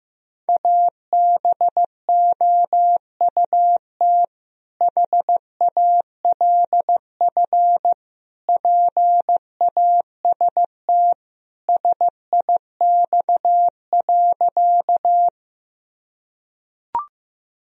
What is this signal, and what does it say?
Morse code, heard as a keyed tone, about 15 wpm, 700 Hz. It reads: ABOUT HALF PAST SIX.
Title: Sentences from Top 1000 Words - Review Part 3 - 15wpm